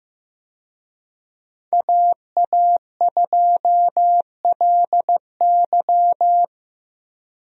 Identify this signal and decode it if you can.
Morse code, heard as a keyed tone, about 16 wpm, 700 Hz. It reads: AA2LY
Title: USA Call Signs - 5 Characters - Repeated 3x - 15wpm